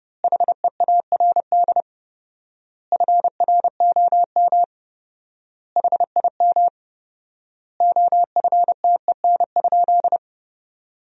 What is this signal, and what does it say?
Morse code, heard as a keyed tone, about 30 wpm, 700 Hz. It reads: HEARD FROM HIM OFTEN?
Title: 4x Word-Spacing - Sentences from Top 400 Words - 30wpm